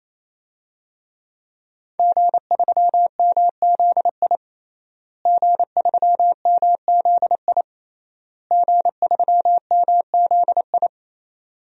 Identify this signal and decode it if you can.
Morse code, heard as a keyed tone, about 28 wpm, 700 Hz. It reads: G3MZI G3MZI G3MZI
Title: DXCC-100 Call Signs - 5 Characters - Repeated 3x - 28wpm